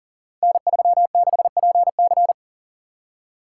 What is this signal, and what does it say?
Morse code, heard as a keyed tone, about 39 wpm, 700 Hz. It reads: N36PC
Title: Code Groups - 5 Letter:Numbers - 40wpm